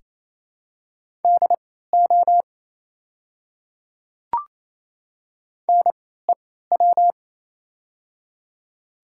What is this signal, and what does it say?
Morse code, heard as a keyed tone, about 28 wpm, spaced as for 16 wpm, 690 Hz. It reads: DO NEW
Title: 3x Character Spacing - Top 300 Words - 28wpm